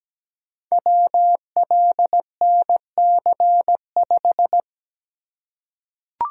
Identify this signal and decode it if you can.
Morse code, heard as a keyed tone, about 17 wpm, 700 Hz. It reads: WLNC5
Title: Code Groups - 5 Letter:Numbers - 17wpm